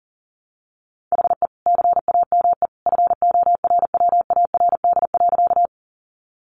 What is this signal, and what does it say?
Morse code, heard as a keyed tone, about 37 wpm, 700 Hz. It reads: HE CAME FORWARD.
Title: Sentences from Top 1300 Words 40wpm